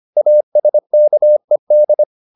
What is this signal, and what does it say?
Morse code, heard as a keyed tone, about 25 wpm, 600 Hz. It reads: ASKED